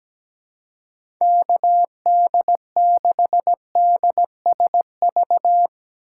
Morse code, heard as a keyed tone, about 17 wpm, 700 Hz. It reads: KD6DSV
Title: USA Call Signs - 3x Character Spacing - 17wpm